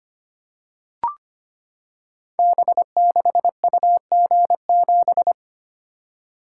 Morse code, heard as a keyed tone, about 25 wpm, 700 Hz. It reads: B6UG7
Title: Code Groups - 5 Letter:Numbers - 25wpm